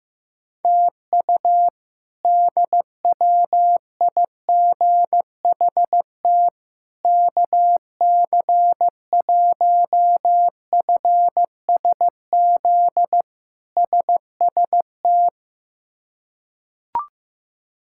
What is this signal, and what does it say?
Morse code, heard as a keyed tone, about 15 wpm, 700 Hz. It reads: TU DWIGHT KC1FSZ SST
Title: K1USN SST QSO Practice - 15wpm